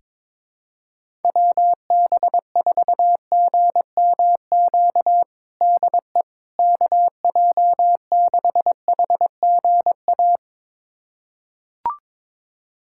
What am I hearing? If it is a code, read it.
Morse code, heard as a keyed tone, about 22 wpm, 700 Hz. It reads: WB4GMQ DE KJ6HGA
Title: USA Callsign DE Callsign - 6 Characters - 22wpm